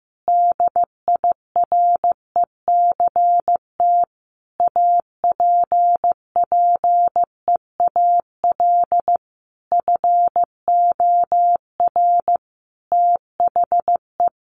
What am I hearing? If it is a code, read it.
Morse code, heard as a keyed tone, about 15 wpm, 700 Hz. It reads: DIRECT APPEAL FOR THE
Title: Sentences from Top 1300 Words 15wpm